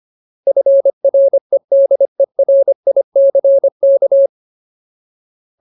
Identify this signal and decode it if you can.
Morse code, heard as a keyed tone, about 25 wpm, 550 Hz. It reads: FREDERICK
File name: Ultimate Word List / 25WPM Ultimate